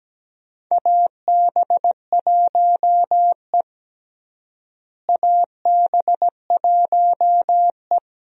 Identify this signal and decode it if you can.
Morse code, heard as a keyed tone, about 17 wpm, 700 Hz. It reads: AB1E AB1E
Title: USA Call Signs - 4 Characters - Repeated 3x - 17wpm